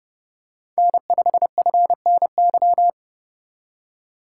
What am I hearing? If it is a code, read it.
Morse code, heard as a keyed tone, about 32 wpm, 700 Hz. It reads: N5FNY
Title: USA Call Signs - 5 Characters - 3x Character Spacing - 30wpm